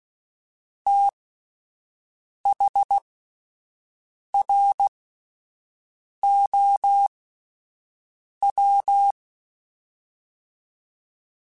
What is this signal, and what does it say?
Morse code, heard as a keyed tone, about 16 wpm, spaced as for 5 wpm, 800 Hz. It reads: THROW